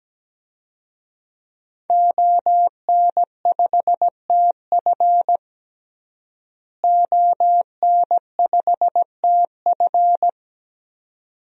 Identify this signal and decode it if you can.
Morse code, heard as a keyed tone, about 17 wpm, 700 Hz. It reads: ON5TF ON5TF
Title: DXCC-100 Call Signs - 5 Characters - Repeated 2x - 17wpm